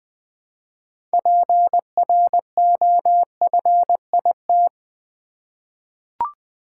Morse code, heard as a keyed tone, about 20 wpm, 700 Hz. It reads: PROFIT
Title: Six Letter Words - 20wpm